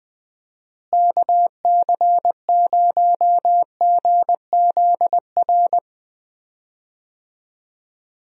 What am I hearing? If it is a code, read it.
Morse code, heard as a keyed tone, about 20 wpm, 700 Hz. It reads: KC0GZR
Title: USA Call Signs - 6 Characters - Repeated 2x - 20wpm